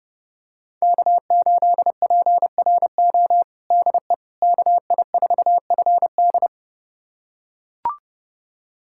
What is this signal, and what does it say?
Morse code, heard as a keyed tone, about 30 wpm, 700 Hz. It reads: K8PRO DE KI4FD
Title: USA Callsign DE Callsign - 5 Characters - 30wpm